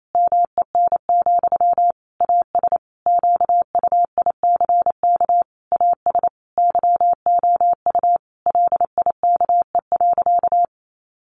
Morse code, heard as a keyed tone, about 28 wpm, 700 Hz. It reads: MEN, AS QUICK AS YOU LIKE.
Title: Sentences from Top 500 Words 28wpm